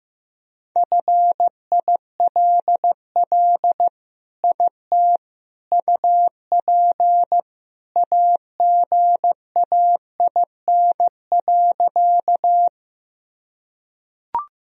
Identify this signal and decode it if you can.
Morse code, heard as a keyed tone, about 15 wpm, 700 Hz. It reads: FILL IT UP AGAIN.
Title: Sentences from Top 1000 Words - Review Part 3 - 15wpm